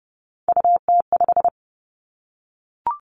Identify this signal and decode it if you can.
Morse code, heard as a keyed tone, about 29 wpm, 700 Hz. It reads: UT5
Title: Callsign Prefixes for Top 25 DXCC - Repeated 3x - 30wpm